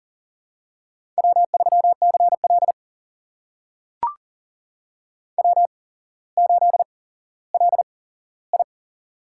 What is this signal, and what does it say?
Morse code, heard as a keyed tone, about 38 wpm, spaced as for 17 wpm, 700 Hz. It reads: W3CL W8LI
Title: USA Call Signs - 4 Characters - 8x Character Spacing - 40wpm